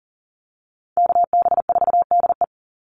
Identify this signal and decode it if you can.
Morse code, heard as a keyed tone, about 40 wpm, 700 Hz. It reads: KB4DE
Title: Call Signs - 40wpm